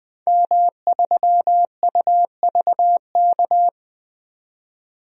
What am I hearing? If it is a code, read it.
Morse code, heard as a keyed tone, about 21 wpm, 700 Hz. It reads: M3UVK
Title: DXCC-100 Call Signs - 5 Characters - 20wpm